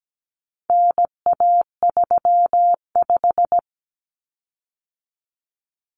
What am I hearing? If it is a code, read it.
Morse code, heard as a keyed tone, about 17 wpm, 700 Hz. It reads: NA35
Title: Common Cut Numbers - 4-Digit 17wpm